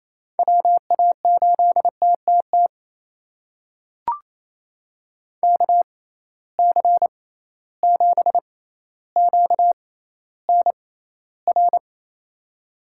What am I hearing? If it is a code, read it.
Morse code, heard as a keyed tone, about 28 wpm, spaced as for 14 wpm, 700 Hz. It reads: WA8TTT KC7QNR
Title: USA Call Signs - 6 Characters - 6x Character Spacing - 28wpm